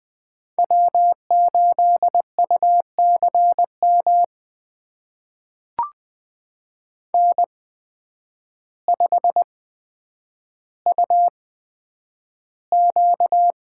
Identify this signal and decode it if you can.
Morse code, heard as a keyed tone, about 20 wpm, spaced as for 9 wpm, 700 Hz. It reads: W8UCM N5UQ
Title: USA Call Signs - 5 Characters - 8x Character Spacing - 20wpm